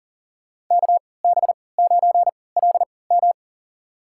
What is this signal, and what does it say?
Morse code, heard as a keyed tone, about 40 wpm, spaced as for 23 wpm, 700 Hz. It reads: KB9LM
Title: USA Call Signs - 5 Characters - 3x Character Spacing - 40wpm